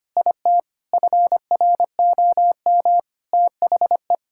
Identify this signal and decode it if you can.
Morse code, heard as a keyed tone, about 25 wpm, 700 Hz. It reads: IT FROM THE
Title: Sentences from Top 1000 Words - Review Part 5 - 25wpm